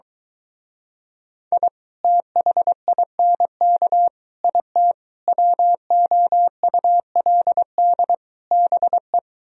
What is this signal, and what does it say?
Morse code, heard as a keyed tone, about 23 wpm, 700 Hz. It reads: I THINK IT WOULD BE